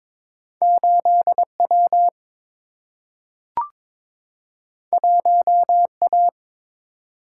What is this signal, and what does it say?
Morse code, heard as a keyed tone, about 22 wpm, 700 Hz. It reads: 8W 1A
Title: Single-Letter Single-Number 22wpm